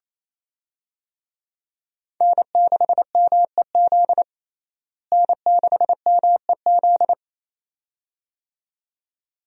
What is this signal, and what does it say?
Morse code, heard as a keyed tone, about 27 wpm, 700 Hz. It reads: N6MEZ N6MEZ
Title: USA Call Signs - 5 Characters - Repeated 2x - 28wpm